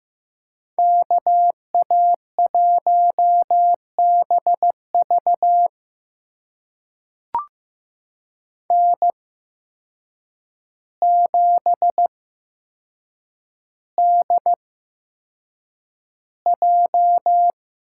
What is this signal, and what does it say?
Morse code, heard as a keyed tone, about 15 wpm, spaced as for 7 wpm, 700 Hz. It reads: KA1BV N7DJ
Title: USA Call Signs - 5 Characters - 8x Character Spacing - 15wpm